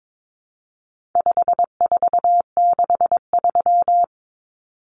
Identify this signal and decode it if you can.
Morse code, heard as a keyed tone, about 22 wpm, 700 Hz. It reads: 5463
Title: Numbers - 4-Digit 22wpm